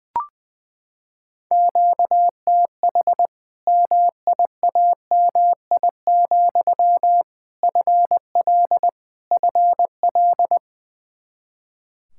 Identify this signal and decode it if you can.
Morse code, heard as a keyed tone, about 20 wpm, 700 Hz. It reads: QTH MIAMI, FL FL
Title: Practice QSO - Exchange 1 - Beginning - 20wpm